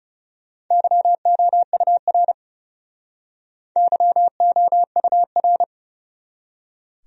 Morse code, heard as a keyed tone, about 41 wpm, 700 Hz. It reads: YOUR YOUR